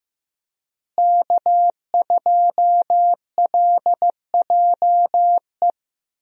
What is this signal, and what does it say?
Morse code, heard as a keyed tone, about 15 wpm, 700 Hz. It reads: K2LJE
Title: USA Call Signs - 5 Characters - Repeated 3x - 15wpm